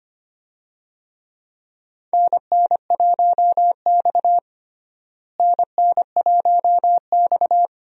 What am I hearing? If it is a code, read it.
Morse code, heard as a keyed tone, about 25 wpm, 700 Hz. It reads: NN1X NN1X
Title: USA Call Signs - 4 Characters - Repeated 2x - 25wpm